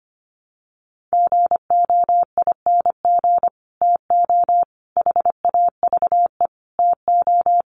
Morse code, heard as a keyed tone, about 25 wpm, 700 Hz. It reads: GOING TO HAVE TO
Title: Sets of 5 Words Encore2 25wpm